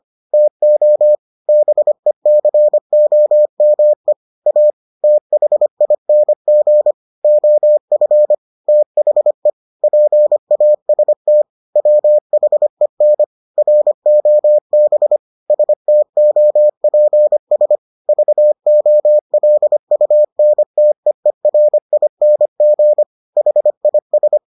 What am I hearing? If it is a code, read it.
Morse code, heard as a keyed tone, about 25 wpm, 600 Hz. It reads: TO BECOME A THING OF THE PAST WHEN ROB STOPS VOLUNTEERING HIS